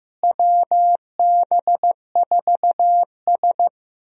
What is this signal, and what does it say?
Morse code, heard as a keyed tone, about 15 wpm, 700 Hz. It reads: WB4S
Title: USA Call Signs - 4 Characters - Repeated 2x - 15wpm